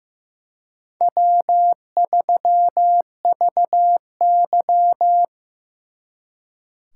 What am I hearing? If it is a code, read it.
Morse code, heard as a keyed tone, about 18 wpm, 700 Hz. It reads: W3VY
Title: Speed Racing - USA Call Signs 15wpm